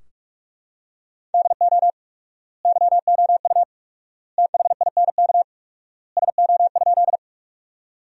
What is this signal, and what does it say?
Morse code, heard as a keyed tone, about 45 wpm, 700 Hz. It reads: DO YOU THINK SO?
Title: 4x Word-Spacing - Sentences from Top 200 Words - 45wpm